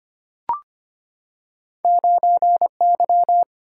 Morse code, heard as a keyed tone, about 25 wpm, 700 Hz. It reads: 9Y
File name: Two Letter-Number Combinations - 25wpm